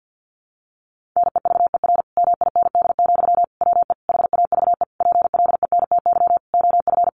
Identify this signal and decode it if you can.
Morse code, heard as a keyed tone, about 50 wpm, 700 Hz. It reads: NEVER MIND, WE HAVE PLENTY OF
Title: Sentences from Top 1300 Words 50wpm